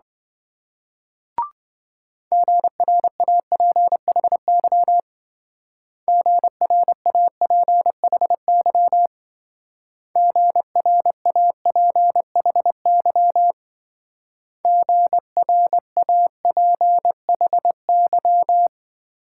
Morse code, heard as a keyed tone, about 25 wpm, 700 Hz. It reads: GRAPHY GRAPHY GRAPHY GRAPHY